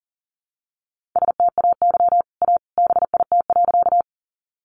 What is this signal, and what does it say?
Morse code, heard as a keyed tone, about 40 wpm, 700 Hz. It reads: STAY A BIT.
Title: Sentences from Top 900 Words 40wpm